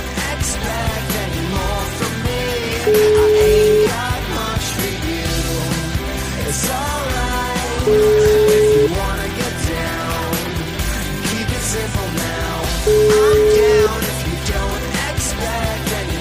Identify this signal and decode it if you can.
Ringback tone (Europe, 425 Hz), with noise.